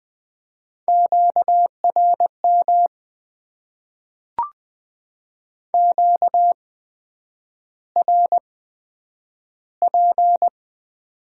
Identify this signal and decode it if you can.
Morse code, heard as a keyed tone, about 20 wpm, spaced as for 9 wpm, 700 Hz. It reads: QRM QRP
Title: Top 90 QSO Elements - 8x Character Spacing - 20wpm